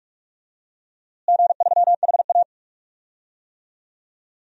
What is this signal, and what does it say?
Morse code, heard as a keyed tone, about 46 wpm, 700 Hz. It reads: G3HA
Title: DXCC-100 Call Signs - 4 Characters - Repeated 3x - 45wpm